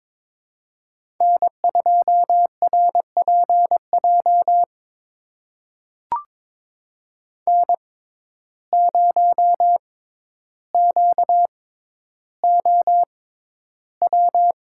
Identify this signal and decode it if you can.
Morse code, heard as a keyed tone, about 22 wpm, spaced as for 11 wpm, 700 Hz. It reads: N2RPJ N0QOW
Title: USA Call Signs - 5 Characters - 6x Character Spacing - 22wpm